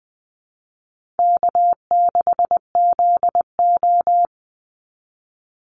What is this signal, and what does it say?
Morse code, heard as a keyed tone, about 20 wpm, 700 Hz. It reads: K6ZO
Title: Easy Call Signs - 20wpm